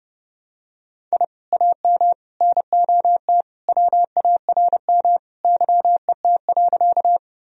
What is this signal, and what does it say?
Morse code, heard as a keyed tone, about 30 wpm, 700 Hz. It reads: I AM NOT WARM YET.
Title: Sentences from Top 1000 Words - Review Part 2 - 30wpm